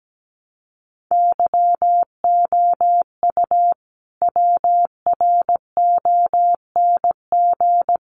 Morse code, heard as a keyed tone, about 17 wpm, 700 Hz. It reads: YOU WRONG
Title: Sentences from Top 900 Words 17wpm